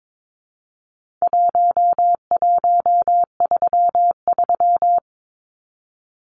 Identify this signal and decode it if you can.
Morse code, heard as a keyed tone, about 22 wpm, 700 Hz. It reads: 1133
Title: Numbers - 4-Digit 22wpm